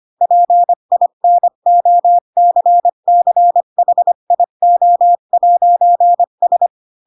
Morse code, heard as a keyed tone, about 25 wpm, 700 Hz. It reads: PINOCCHIO'S
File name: Pinocchio / Pinocchio ch4 20 25 ik1hge